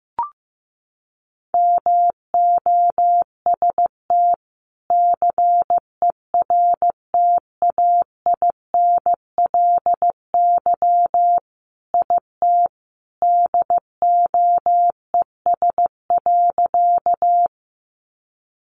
Morse code, heard as a keyed tone, about 15 wpm, 700 Hz. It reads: MOST CERTAINLY IT DOES.
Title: Sentences from Top 900 Words 15wpm